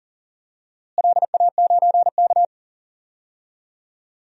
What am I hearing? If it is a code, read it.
Morse code, heard as a keyed tone, about 40 wpm, 700 Hz. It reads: LA9K
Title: DXCC-100 Call Signs - 4 Characters - Repeated 2x - 40wpm